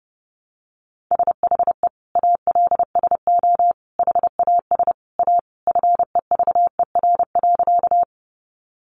Morse code, heard as a keyed tone, about 30 wpm, 700 Hz. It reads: SHE ALSO HAS A FEVER.
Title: Easy Phrases 30wpm